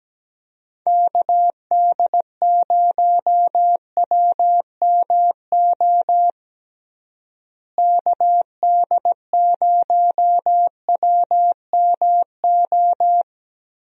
Morse code, heard as a keyed tone, about 17 wpm, 700 Hz. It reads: KD0WMO KD0WMO
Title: USA Call Signs - 6 Characters - Repeated 3x - 17wpm